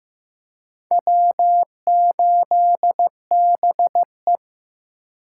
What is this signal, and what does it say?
Morse code, heard as a keyed tone, about 18 wpm, 700 Hz. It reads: W8BE
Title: Speed Racing - USA Call Signs 15wpm